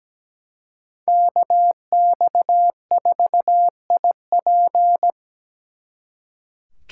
Morse code, heard as a keyed tone, about 20 wpm, 700 Hz. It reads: KX4IP
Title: Speed Racing - USA 2x2 Call Signs 17wpm